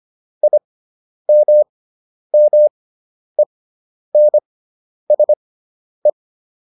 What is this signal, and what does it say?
Morse code, heard as a keyed tone, about 25 wpm, spaced as for 10 wpm, 600 Hz. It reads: IMMENSE